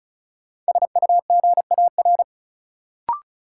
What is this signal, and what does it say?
Morse code, heard as a keyed tone, about 34 wpm, 700 Hz. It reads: SUGAR
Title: Sentences from Top 1200 Words - New Word Drill - 35wpm